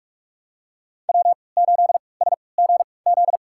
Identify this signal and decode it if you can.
Morse code, heard as a keyed tone, about 46 wpm, spaced as for 26 wpm, 700 Hz. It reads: W8SGZ
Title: USA Call Signs - 5 Characters - 3x Character Spacing - 45wpm